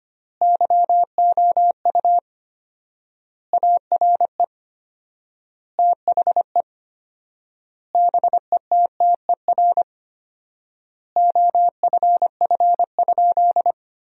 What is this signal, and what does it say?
Morse code, heard as a keyed tone, about 25 wpm, 700 Hz. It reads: YOU ARE THE BETTER OFF?
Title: 4x Word-Spacing - Sentences from Top 500 Words - 25wpm